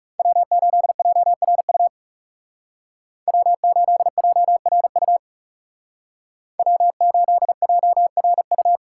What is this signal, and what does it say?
Morse code, heard as a keyed tone, about 38 wpm, 700 Hz. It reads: W8JRU W8JRU W8JRU